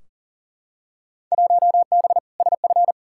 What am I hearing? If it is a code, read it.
Morse code, heard as a keyed tone, about 40 wpm, 700 Hz. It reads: 1B SF
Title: Field Day Exchanges Encore - 40wpm